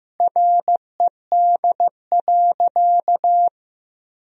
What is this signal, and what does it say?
Morse code, heard as a keyed tone, about 15 wpm, 700 Hz. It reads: RED.
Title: Sentences from Top 1000 Words - Review Part 2 - 15wpm